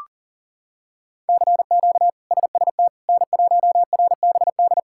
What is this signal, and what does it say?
Morse code, heard as a keyed tone, about 40 wpm, 700 Hz. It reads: CQ SST N1RBD
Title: K1USN SST QSO Practice - 40wpm